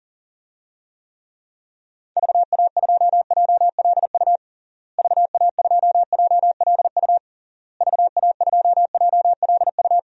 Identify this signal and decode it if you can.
Morse code, heard as a keyed tone, about 40 wpm, 700 Hz. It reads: VA2JLU VA2JLU VA2JLU
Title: DXCC-100 Call Signs - 6 Characters - Repeated 3x - 40wpm